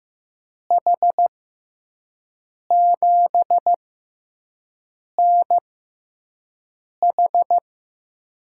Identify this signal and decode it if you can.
Morse code, heard as a keyed tone, about 15 wpm, spaced as for 5 wpm, 700 Hz. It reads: H7NH
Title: USA Call Signs - 5 Characters - 6x Character Spacing - 15wpm